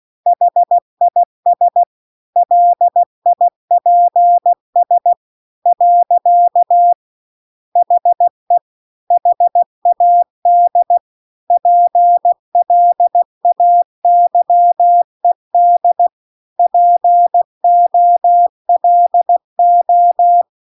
Morse code, heard as a keyed tone, about 16 wpm, 700 Hz. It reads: HIS LIPS . HE HAD PLAYED POLO